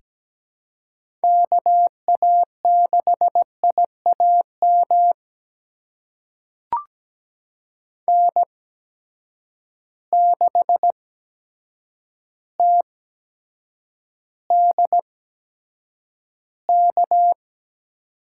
Morse code, heard as a keyed tone, about 17 wpm, spaced as for 8 wpm, 700 Hz. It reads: KA6IAM N6TDK
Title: USA Call Signs - 8x Character Spacing - 17wpm